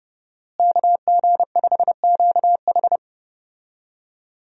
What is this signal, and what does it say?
Morse code, heard as a keyed tone, about 30 wpm, 700 Hz. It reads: KG5QH
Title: USA Call Signs - 5 Characters - Repeated 3x - 30wpm